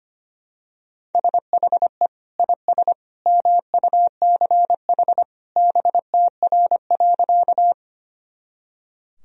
Morse code, heard as a keyed tone, about 25 wpm, 700 Hz. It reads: SHE IS MUCH BTR.
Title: Sentences from Top 400 Words and Abbreviations 25wpm